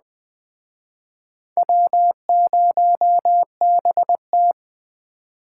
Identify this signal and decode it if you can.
Morse code, heard as a keyed tone, about 21 wpm, 700 Hz. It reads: W0BT